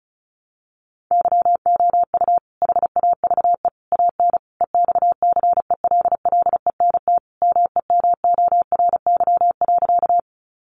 Morse code, heard as a keyed tone, about 34 wpm, 700 Hz. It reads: YOU HAVE AN EXCELLENT MEMORY.
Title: Sentences from Top 1300 Words 35wpm